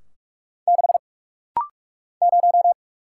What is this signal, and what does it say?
Morse code, heard as a keyed tone, about 48 wpm, 690 Hz. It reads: 6 0
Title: Single Letter-Number - Rapid-Fire 45wpm